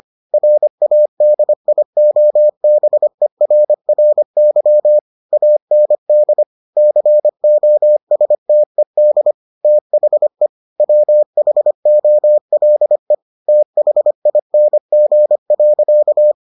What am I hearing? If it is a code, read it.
Morse code, heard as a keyed tone, about 25 wpm, 600 Hz. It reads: RADIOBERRY AND COSTED THE WHOLE THING.